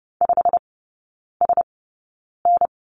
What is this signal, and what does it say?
Morse code, heard as a keyed tone, about 30 wpm, spaced as for 9 wpm, 700 Hz. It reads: 5SN
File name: course / 30:10wpm / 087 Lesson 15e - Number 5 - Three Characters 30wpm